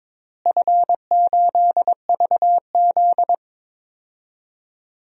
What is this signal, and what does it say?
Morse code, heard as a keyed tone, about 22 wpm, 700 Hz. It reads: F8VZ
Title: DXCC-100 Call Signs - 4 Characters - Repeated 3x - 22wpm